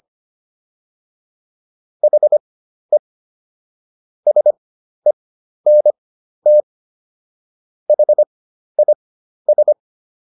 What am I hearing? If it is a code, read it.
Morse code, heard as a keyed tone, about 25 wpm, spaced as for 12 wpm, 600 Hz. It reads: HE SENT HIS